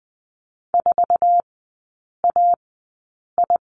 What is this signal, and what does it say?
Morse code, heard as a keyed tone, about 20 wpm, spaced as for 8 wpm, 700 Hz. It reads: 4AI
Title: 067 Lesson 12e - Letter d - Three Characters 20wpm